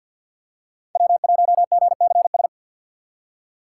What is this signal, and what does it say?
Morse code, heard as a keyed tone, about 52 wpm, 700 Hz. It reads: W1GCS
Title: K1USN SST QSO Practice - 50wpm